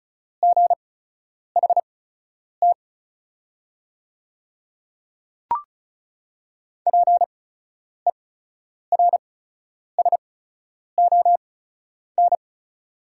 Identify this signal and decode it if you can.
Morse code, heard as a keyed tone, about 36 wpm, spaced as for 10 wpm, 700 Hz. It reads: GHT PERSON
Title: 8x Character Spacing - Top 500 Words - 35wpm